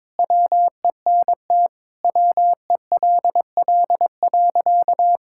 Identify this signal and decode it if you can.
Morse code, heard as a keyed tone, about 22 wpm, 700 Hz. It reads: WENT WELL.
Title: Sentences from Top 1600 Words 22wpm